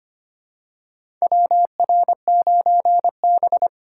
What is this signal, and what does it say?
Morse code, heard as a keyed tone, about 25 wpm, 700 Hz. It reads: WR9B
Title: USA Call Signs - 2x Character Spacing - 25wpm